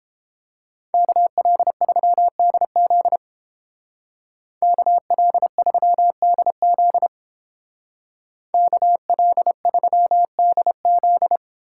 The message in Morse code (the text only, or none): KL3DZ KL3DZ KL3DZ